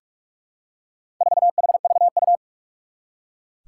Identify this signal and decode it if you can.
Morse code, heard as a keyed tone, about 44 wpm, 700 Hz. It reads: 4HVU
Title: Four Challenging Letter-Number Combinations 45wpm